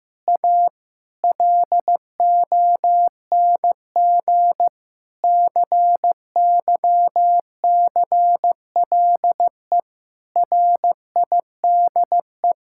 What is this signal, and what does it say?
Morse code, heard as a keyed tone, about 15 wpm, 700 Hz. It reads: A LONG CYCLE RIDE
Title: Sentences from Top 1000 Words - Review Part 6 - 15wpm